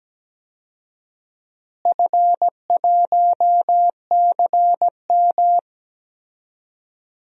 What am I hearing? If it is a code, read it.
Morse code, heard as a keyed tone, about 17 wpm, 700 Hz. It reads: F1CM
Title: DXCC-100 Call Signs - 4 Characters - Repeated 2x - 17wpm